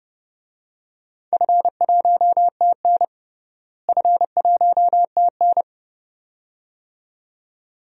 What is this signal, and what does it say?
Morse code, heard as a keyed tone, about 30 wpm, 700 Hz. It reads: F1TN F1TN
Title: DXCC-100 Call Signs - 4 Characters - Repeated 2x - 30wpm